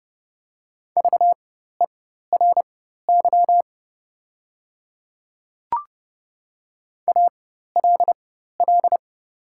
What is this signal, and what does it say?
Morse code, heard as a keyed tone, about 30 wpm, spaced as for 14 wpm, 700 Hz. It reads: VERY ALL
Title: 4x Character Spacing - Top 200 Words - 30wpm